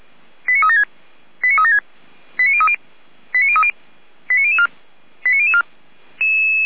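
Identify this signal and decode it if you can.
Five-tone selective call (ZVEI), heard as quick runs of five tones, tones of about 80 ms. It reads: 89376 89376 89030 89030 89004 89004
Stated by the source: ZVEI five-tone selective call received on an SDR, from the Feuerwehr Orlamünde audio archive.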